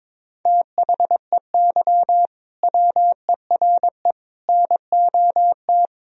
Morse code, heard as a keyed tone, about 22 wpm, 700 Hz. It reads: THEY WERE NOT
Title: Sentences from Top 1000 Words - Review Part 3 - 22wpm